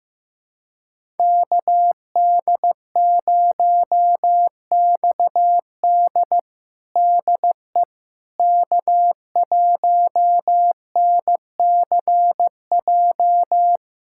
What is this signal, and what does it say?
Morse code, heard as a keyed tone, about 16 wpm, 700 Hz. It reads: KD0XD DE K1NCJ
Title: USA Callsign DE Callsign - 5 Characters - 15wpm